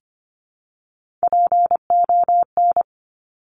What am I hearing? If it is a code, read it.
Morse code, heard as a keyed tone, about 25 wpm, 700 Hz. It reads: PON
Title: Top Three Letter Combinations in Words 25wpm